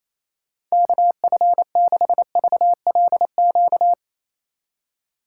Morse code, heard as a keyed tone, about 28 wpm, 700 Hz. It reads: KF6VLQ
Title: USA Call Signs - 6 Characters - 2x Character Spacing - 28wpm